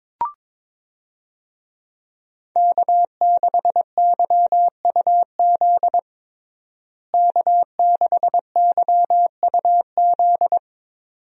Morse code, heard as a keyed tone, about 21 wpm, 700 Hz. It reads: K6YUZ K6YUZ
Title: USA Call Signs - 5 Characters - Repeated 2x - 22wpm